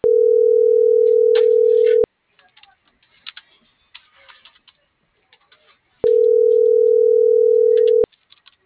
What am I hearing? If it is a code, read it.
Ringback tone (North America), 440 + 480 Hz.